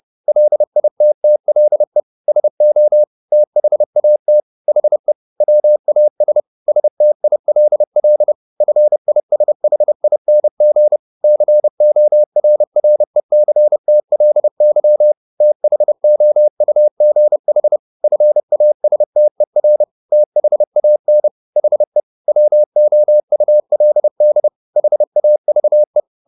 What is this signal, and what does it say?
Morse code, heard as a keyed tone, about 30 wpm, 600 Hz. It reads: LITTLE SO THAT HE WAS STILL FISHING CORRECTLY THOUGH FASTER THAN HE WOULD HAVE
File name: the old man and the sea / the old man and the sea 1 30wpm